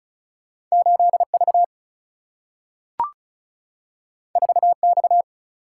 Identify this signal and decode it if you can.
Morse code, heard as a keyed tone, about 36 wpm, 700 Hz. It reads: 8V 4X
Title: Single-Letter Single-Number 35wpm